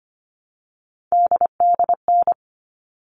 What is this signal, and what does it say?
Morse code, heard as a keyed tone, about 25 wpm, 700 Hz. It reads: DDN
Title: All Cut Numbers - 3-Digit 25wpm